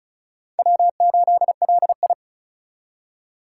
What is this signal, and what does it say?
Morse code, heard as a keyed tone, about 36 wpm, 700 Hz. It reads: W8LI
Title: USA Call Signs - 4 Characters - 4x Character Spacing - 35wpm